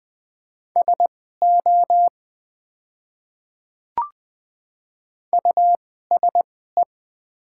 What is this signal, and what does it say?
Morse code, heard as a keyed tone, about 20 wpm, spaced as for 15 wpm, 700 Hz. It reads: SO USE
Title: 2x Character Spacing - Top 100 Words - 20wpm